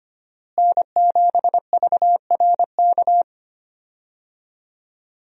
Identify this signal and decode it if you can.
Morse code, heard as a keyed tone, about 25 wpm, 700 Hz. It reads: N7VRK
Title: USA Call Signs - 5 Characters - Repeated 2x - 25wpm